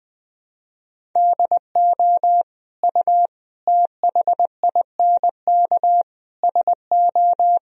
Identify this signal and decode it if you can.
Morse code, heard as a keyed tone, about 20 wpm, 700 Hz. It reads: DO U THINK SO
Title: Sentences from Top 200 Words and Abbreviations 20wpm